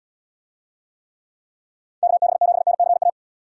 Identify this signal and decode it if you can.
Morse code, heard as a keyed tone, about 75 wpm, 700 Hz. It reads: KB7NCA